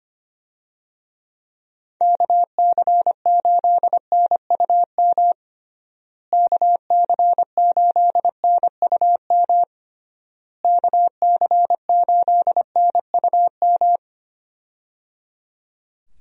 Morse code, heard as a keyed tone, about 25 wpm, 700 Hz. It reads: KC8NUM KC8NUM KC8NUM
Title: USA Call Signs - 6 Characters - Repeated 3x - 25wpm